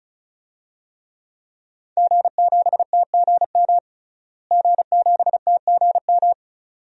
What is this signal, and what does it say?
Morse code, heard as a keyed tone, about 36 wpm, 700 Hz. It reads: G7TGM G7TGM
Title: DXCC-100 Call Signs - 5 Characters - Repeated 2x - 35wpm